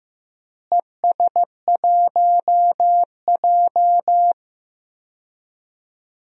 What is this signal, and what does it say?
Morse code, heard as a keyed tone, about 15 wpm, 700 Hz. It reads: ES1J
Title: DXCC-100 Call Signs - 4 Characters - Repeated 2x - 15wpm